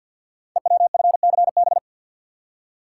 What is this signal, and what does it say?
Morse code, heard as a keyed tone, about 48 wpm, 700 Hz. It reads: EWFCB
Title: Code Groups - 5 Letters - 50wpm